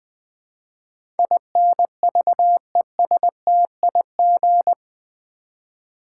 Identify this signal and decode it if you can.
Morse code, heard as a keyed tone, about 20 wpm, 700 Hz. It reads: INVESTIG
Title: Build A Word - 4 Letters At A Time - 12-Letter Words - 20wpm